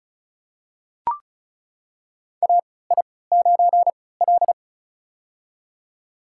Morse code, heard as a keyed tone, about 36 wpm, spaced as for 20 wpm, 700 Hz. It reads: AI9L
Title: USA Call Signs - 4 Characters - 3x Character Spacing - 35wpm